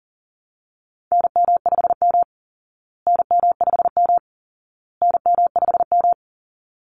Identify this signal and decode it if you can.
Morse code, heard as a keyed tone, about 40 wpm, 700 Hz. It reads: NM5M NM5M NM5M
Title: Call Signs Repeated 2x - 40wpm